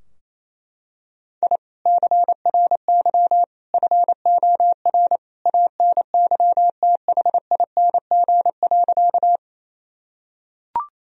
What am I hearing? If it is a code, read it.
Morse code, heard as a keyed tone, about 27 wpm, 700 Hz. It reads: I CRY FOR ANYTHING.
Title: Sentences from Top 1000 Words - Review Part 2 - 28wpm